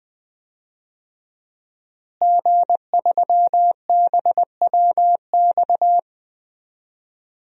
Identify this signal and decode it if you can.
Morse code, heard as a keyed tone, about 20 wpm, 700 Hz. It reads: G3BWX
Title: DXCC-100 Call Signs - 5 Characters - Repeated 3x - 20wpm